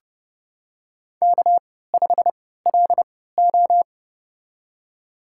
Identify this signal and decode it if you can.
Morse code, heard as a keyed tone, about 30 wpm, spaced as for 17 wpm, 700 Hz. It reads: K5LO
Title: USA Call Signs - 4 Characters - 3x Character Spacing - 30wpm